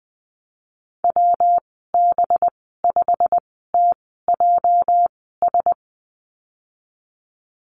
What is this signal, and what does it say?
Morse code, heard as a keyed tone, about 20 wpm, spaced as for 15 wpm, 700 Hz. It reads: WB5TJS
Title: 231 Lesson 37g - Letter x - Call Signs 20wpm